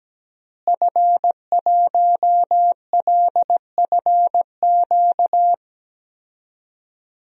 Morse code, heard as a keyed tone, about 17 wpm, 700 Hz. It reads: F1LFQ
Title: DXCC-100 Call Signs - 5 Characters - Repeated 3x - 17wpm